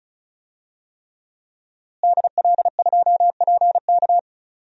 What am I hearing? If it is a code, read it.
Morse code, heard as a keyed tone, about 34 wpm, 700 Hz. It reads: DL2PK
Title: DXCC-100 Call Signs - 5 Characters - Repeated 3x - 35wpm